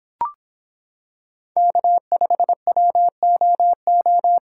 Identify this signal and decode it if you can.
Morse code, heard as a keyed tone, about 25 wpm, 700 Hz. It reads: K5WOO